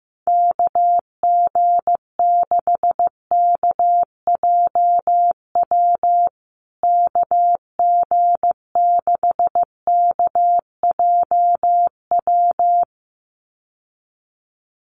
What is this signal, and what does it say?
Morse code, heard as a keyed tone, about 14 wpm, 700 Hz. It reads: KG6KJW KG6KJW
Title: USA Call Signs Repeated 2x 15wpm